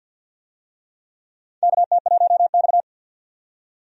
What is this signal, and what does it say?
Morse code, heard as a keyed tone, about 49 wpm, 700 Hz. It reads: KT1X
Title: USA Call Signs - 4 Characters - 50wpm